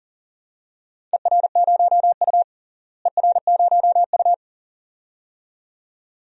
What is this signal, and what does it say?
Morse code, heard as a keyed tone, about 40 wpm, 700 Hz. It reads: ER0U ER0U
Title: DXCC-100 Call Signs - 4 Characters - Repeated 2x - 40wpm